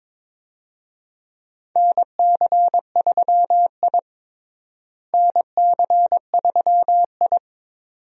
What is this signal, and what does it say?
Morse code, heard as a keyed tone, about 22 wpm, 700 Hz. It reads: NC3I NC3I
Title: USA Call Signs - 4 Characters - Repeated 2x - 22wpm